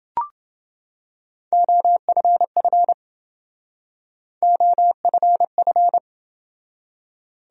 Morse code, heard as a keyed tone, about 27 wpm, 700 Hz. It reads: OFF OFF